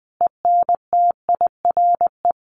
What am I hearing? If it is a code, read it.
Morse code, heard as a keyed tone, about 19 wpm, 700 Hz. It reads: ENTIRE
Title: A Sound Of Thunder 20wpm